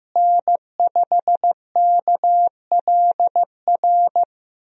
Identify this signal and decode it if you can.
Morse code, heard as a keyed tone, about 15 wpm, 700 Hz. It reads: N5KLR
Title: USA Call Signs - 5 Characters - Repeated 3x - 15wpm